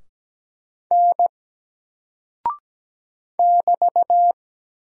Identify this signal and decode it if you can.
Morse code, heard as a keyed tone, about 17 wpm, 700 Hz. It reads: N <BT>
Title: Single Character (Letter:Number:Punctuation:Prosign) Alt 17wpm